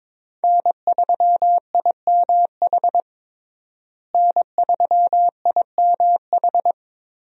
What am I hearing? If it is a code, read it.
Morse code, heard as a keyed tone, about 22 wpm, 700 Hz. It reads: N3IMH N3IMH
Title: USA Call Signs - 5 Characters - Repeated 2x - 22wpm